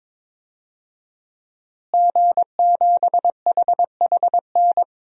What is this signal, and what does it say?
Morse code, heard as a keyed tone, about 22 wpm, 700 Hz. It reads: G7HHN
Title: DXCC-100 Call Signs - 5 Characters - Repeated 2x - 22wpm